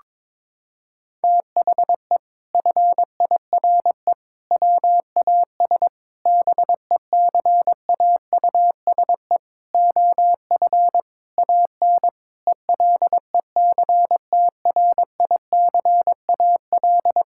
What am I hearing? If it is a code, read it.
Morse code, heard as a keyed tone, about 22 wpm, 700 Hz. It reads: THE FIRE WAS BECAUSE OF AN ELECTRICAL
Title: Sentences from Top 1000 Words - Review Part 6 - 22wpm